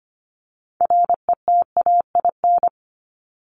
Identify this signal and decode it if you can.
Morse code, heard as a keyed tone, about 25 wpm, 700 Hz. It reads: RETAIN
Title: Top 1751-2000 Words 25wpm